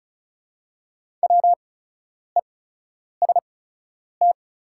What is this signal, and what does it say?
Morse code, heard as a keyed tone, about 36 wpm, spaced as for 10 wpm, 700 Hz. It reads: WEST
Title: 8x Character Spacing - Top 500 Words - 35wpm